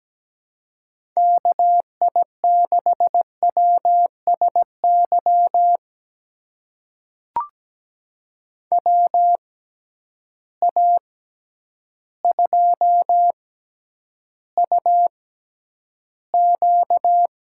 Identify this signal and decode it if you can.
Morse code, heard as a keyed tone, about 17 wpm, spaced as for 9 wpm, 700 Hz. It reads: KI6WSY WA2UQ
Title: USA Call Signs - 6 Characters - 6x Character Spacing - 17wpm